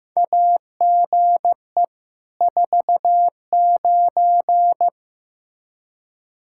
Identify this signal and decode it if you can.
Morse code, heard as a keyed tone, about 15 wpm, 700 Hz. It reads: AGE 49
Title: K1USN SST QSO Practice 15wpm